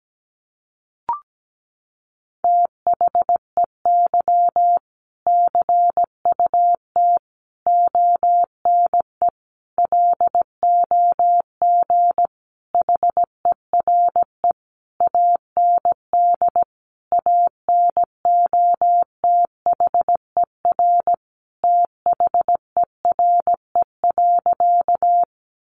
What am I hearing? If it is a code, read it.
Morse code, heard as a keyed tone, about 17 wpm, 700 Hz. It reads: THEY CUT ONE LOG HERE AND ANOTHER THERE.
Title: Sentences from Top 1400 Words 17wpm